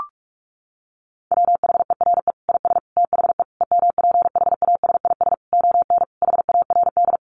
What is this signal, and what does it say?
Morse code, heard as a keyed tone, about 44 wpm, 700 Hz. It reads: WHERE IS THE EMPHASIS ON HARD
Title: Sentences from Top 1400 Words 45wpm